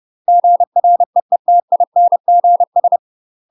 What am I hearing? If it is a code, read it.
Morse code, heard as a keyed tone, about 30 wpm, 700 Hz. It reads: GREETINGS
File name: UK MASTER QSOs TRAINING / 30WPM / UK MASTER QSOs 0046